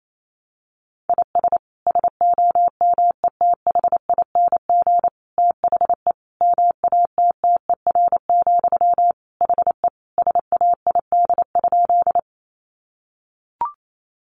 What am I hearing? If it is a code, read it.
Morse code, heard as a keyed tone, about 28 wpm, 700 Hz. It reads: IS SOMETHING THE MATTER, HE SAID?
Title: Sentences from Top 700 Words 28wpm